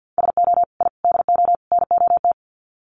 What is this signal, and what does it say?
Morse code, heard as a keyed tone, about 50 wpm, 700 Hz. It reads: SO I DO NOT
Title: Sets of 4 Words 50wpm